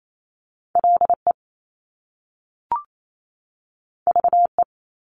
Morse code, heard as a keyed tone, about 28 wpm, 700 Hz. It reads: LE VE
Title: Top Two Letter Combinations in Words 28wpm